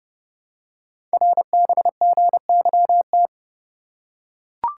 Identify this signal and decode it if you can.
Morse code, heard as a keyed tone, about 30 wpm, 700 Hz. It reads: RBGYT
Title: Code Groups - 5 Letters - 30wpm